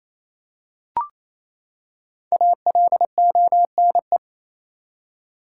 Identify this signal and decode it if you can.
Morse code, heard as a keyed tone, about 28 wpm, 700 Hz. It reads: ALONE
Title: Five Letter Words - 28wpm